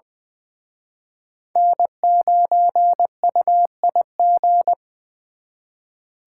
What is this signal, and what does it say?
Morse code, heard as a keyed tone, about 24 wpm, 700 Hz. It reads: N9UIG
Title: Speed Racing - USA 1x3 Call Signs 20wpm